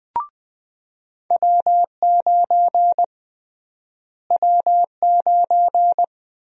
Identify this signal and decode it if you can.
Morse code, heard as a keyed tone, about 20 wpm, 700 Hz. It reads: W9 W9
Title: USA Call Sign Prefixes - Repeated 2x - 20wpm